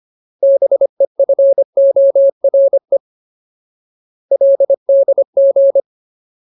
Morse code, heard as a keyed tone, about 25 wpm, 550 Hz. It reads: BEFORE LDG
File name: Ultimate Word List / 25WPM Ultimate